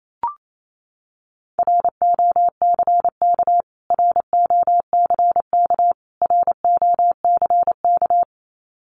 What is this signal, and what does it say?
Morse code, heard as a keyed tone, about 28 wpm, 700 Hz. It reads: ROCK ROCK ROCK
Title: Top 500 Words Repeated 28wpm